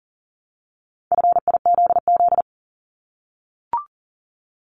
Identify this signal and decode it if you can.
Morse code, heard as a keyed tone, about 40 wpm, 700 Hz. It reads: FIZZ
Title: Multi-letter Phonemes Repeated 3x 40wpm